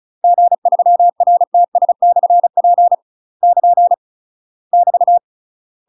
Morse code, heard as a keyed tone, about 36 wpm, 700 Hz. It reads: G3RTS/P <KN> <BT>
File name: UK MASTER QSOs TRAINING / 35WPM / UK MASTER QSOs 0003